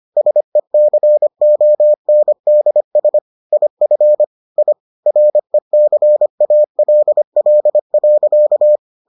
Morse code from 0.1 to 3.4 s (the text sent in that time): SECONDS